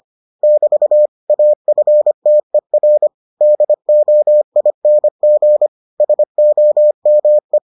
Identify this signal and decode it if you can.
Morse code, heard as a keyed tone, about 25 wpm, 600 Hz. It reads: <BT> AFTER DOING SOME